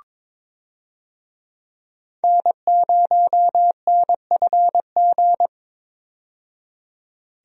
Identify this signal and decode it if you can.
Morse code, heard as a keyed tone, about 22 wpm, 700 Hz. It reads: N0NFG
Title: USA Call Signs - 5 Characters - 22wpm